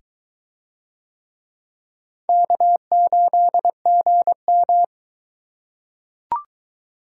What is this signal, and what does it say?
Morse code, heard as a keyed tone, about 22 wpm, 700 Hz. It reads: K8GM